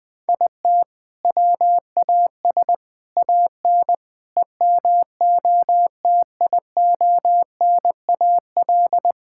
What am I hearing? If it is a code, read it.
Morse code, heard as a keyed tone, about 20 wpm, 700 Hz. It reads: IT WAS AN EMOTIONAL
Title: Sentences from Top 1000 Words - Review Part 5 - 20wpm